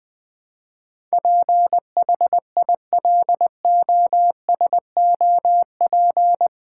Morse code, heard as a keyed tone, about 20 wpm, 700 Hz. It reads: PHILOSOP
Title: Build A Word - 4 Letters At A Time - 10-Letter Words - 20wpm